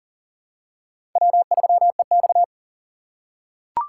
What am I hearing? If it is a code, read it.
Morse code, heard as a keyed tone, about 40 wpm, 700 Hz. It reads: W3EX
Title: USA Call Signs - 4 Characters - 3x Character Spacing - 40wpm